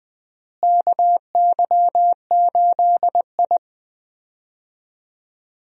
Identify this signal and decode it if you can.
Morse code, heard as a keyed tone, about 20 wpm, 700 Hz. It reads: KY8I
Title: USA Call Signs - 4 Characters - Repeated 3x - 20wpm